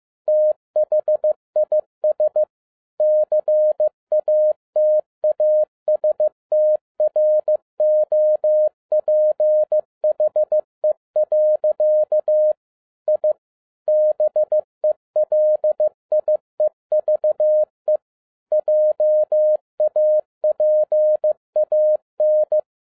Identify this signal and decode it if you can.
Morse code, heard as a keyed tone, about 15 wpm, 610 Hz. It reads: THIS CATASTROPHE. I BELIEVE JAPAN
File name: 2011-07-mp3 / n37 THE 3.11 2 15wpm